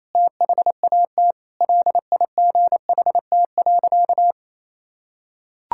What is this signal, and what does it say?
Morse code, heard as a keyed tone, about 28 wpm, 700 Hz. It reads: THAT LIGHT.
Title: Sentences from Top 1000 Words - Review Part 6 - 28wpm